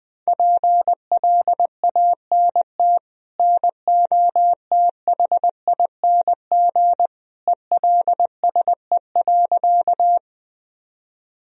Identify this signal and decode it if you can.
Morse code, heard as a keyed tone, about 20 wpm, 700 Hz. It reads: PLANT NOTHING ELSE.
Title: Sentences from Top 1000 Words - Review Part 1 - 20wpm